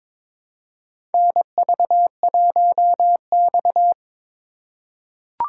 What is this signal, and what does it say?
Morse code, heard as a keyed tone, about 22 wpm, 700 Hz. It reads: NV1X
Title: USA Call Signs - 4 Characters - 2x Character Spacing - 22wpm